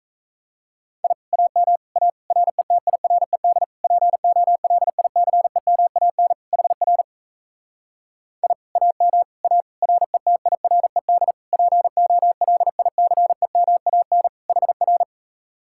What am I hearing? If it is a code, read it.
Morse code, heard as a keyed tone, about 37 wpm, 700 Hz. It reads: I AM A RETIRED POLICEMAN HR I AM A RETIRED POLICEMAN HR